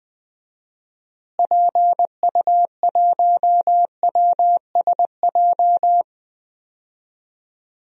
Morse code, heard as a keyed tone, about 21 wpm, 700 Hz. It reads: PU1WSJ
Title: DXCC-100 Call Signs - 6 Characters - Repeated 3x - 20wpm